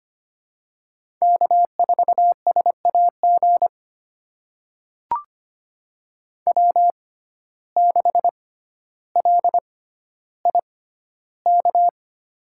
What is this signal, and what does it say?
Morse code, heard as a keyed tone, about 25 wpm, spaced as for 13 wpm, 700 Hz. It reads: K4SAG W6LIK
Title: USA Call Signs - 5 Characters - 6x Character Spacing - 25wpm